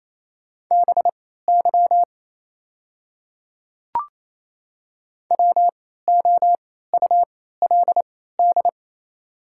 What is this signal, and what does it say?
Morse code, heard as a keyed tone, about 27 wpm, spaced as for 16 wpm, 690 Hz. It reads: BY WOULD
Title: 3x Character Spacing - Top 100 Words - 28wpm